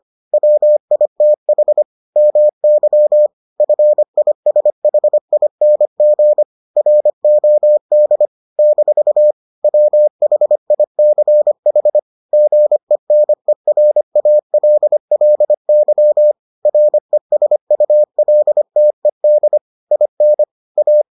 Morse code, heard as a keyed tone, about 25 wpm, 600 Hz. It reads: WITH MY FISHING ROD - WHICH GENERALLY RESULTED IN A